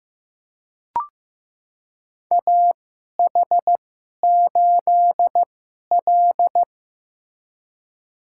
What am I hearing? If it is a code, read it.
Morse code, heard as a keyed tone, about 15 wpm, spaced as for 11 wpm, 700 Hz. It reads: AH8L
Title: USA Call Signs - 4 Characters - 2x Character Spacing - 15wpm